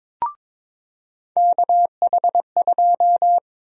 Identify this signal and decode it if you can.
Morse code, heard as a keyed tone, about 22 wpm, 700 Hz. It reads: KH2
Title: USA Call Sign Prefixes - 22wpm